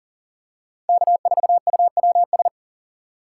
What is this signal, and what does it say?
Morse code, heard as a keyed tone, about 40 wpm, 700 Hz. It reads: K4UWS
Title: USA Call Signs - 5 Characters - 4x Character Spacing - 40wpm